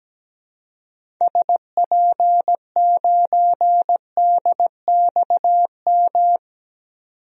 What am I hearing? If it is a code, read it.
Morse code, heard as a keyed tone, about 17 wpm, 700 Hz. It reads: SP9DXM
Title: DXCC-100 Call Signs - 6 Characters - 17wpm